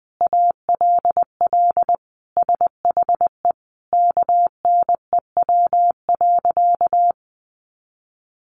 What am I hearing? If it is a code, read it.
Morse code, heard as a keyed tone, about 20 wpm, 700 Hz. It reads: ALL SHE KNEW.
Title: Sentences from Top 1300 Words 20wpm